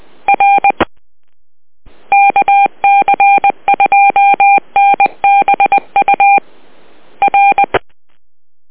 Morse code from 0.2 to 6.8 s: R KC2NBU